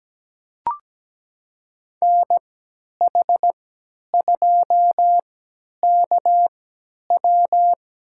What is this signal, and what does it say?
Morse code, heard as a keyed tone, about 17 wpm, spaced as for 10 wpm, 700 Hz. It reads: NH2KW
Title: USA Call Signs - 5 Characters - 3x Character Spacing - 17wpm